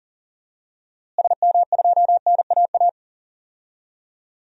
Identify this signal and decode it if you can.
Morse code, heard as a keyed tone, about 40 wpm, 700 Hz. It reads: SM2NAA
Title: DXCC-100 Call Signs - 6 Characters - Repeated 3x - 40wpm